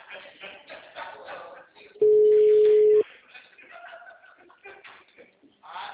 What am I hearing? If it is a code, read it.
Ringback tone (Europe, 425 Hz), with speech.